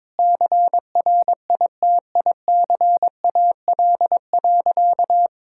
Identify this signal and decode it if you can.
Morse code, heard as a keyed tone, about 22 wpm, 700 Hz. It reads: CRITICAL.
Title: Sentences from Top 1600 Words 22wpm